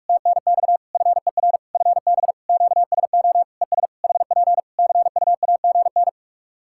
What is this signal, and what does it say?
Morse code, heard as a keyed tone, about 46 wpm, 700 Hz. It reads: TNX FER FB QSO ES HP CUAGN
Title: Practice QSO - Exchange 4 - Ending - 45wpm